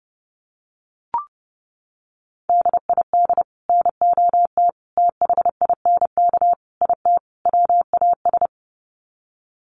Morse code, heard as a keyed tone, about 30 wpm, 700 Hz. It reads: DID NOT THINK IT WAS
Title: Sets of 5 Words Encore 30wpm